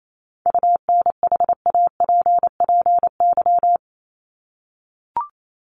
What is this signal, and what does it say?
Morse code, heard as a keyed tone, about 28 wpm, 700 Hz. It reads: UNHAPPY
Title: Top 1501-1750 Words 28wpm